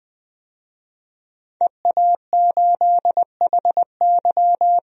Morse code, heard as a keyed tone, about 20 wpm, 700 Hz. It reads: EA8HY
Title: DXCC-100 Call Signs - 5 Characters - 20wpm